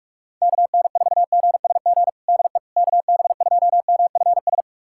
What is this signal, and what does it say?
Morse code, heard as a keyed tone, about 46 wpm, 700 Hz. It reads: KN4GSG DE KB2MFS
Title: USA Callsign DE Callsign - 6 Characters - 45wpm